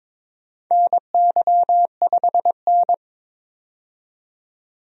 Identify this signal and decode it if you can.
Morse code, heard as a keyed tone, about 22 wpm, 700 Hz. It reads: NY5N
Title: USA Call Signs - 4 Characters - 22wpm